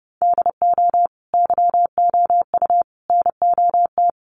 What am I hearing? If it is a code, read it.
Morse code, heard as a keyed tone, about 30 wpm, 700 Hz. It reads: DO YOU NOT
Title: Sentences from Top 1100 Words 30wpm